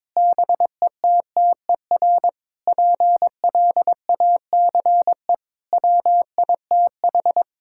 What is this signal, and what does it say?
Morse code, heard as a keyed tone, about 22 wpm, 700 Hz. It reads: BETTER PLACE WITH
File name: Sentences with 7-Word Phrases - 22wpm